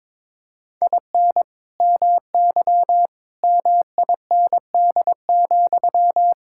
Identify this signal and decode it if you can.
Morse code, heard as a keyed tone, about 22 wpm, 700 Hz. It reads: IN MY MIND,
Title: Sentences from Top 1000 Words - Review Part 6 - 22wpm